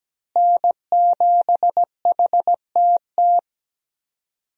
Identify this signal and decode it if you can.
Morse code, heard as a keyed tone, about 17 wpm, 700 Hz. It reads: N7HTT
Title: USA Call Signs - 5 Characters - Repeated 2x - 17wpm